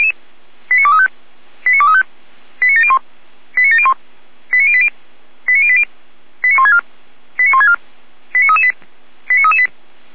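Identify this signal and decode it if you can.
Five-tone selective call (ZVEI), heard as quick runs of five tones, tones of about 70 ms. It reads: 89235 89235 89701 89701 89080 89080 89164 89164 89338 89338
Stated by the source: ZVEI five-tone call received by SDR, fire brigade Orlamünde (feuerwehr-orlamuende).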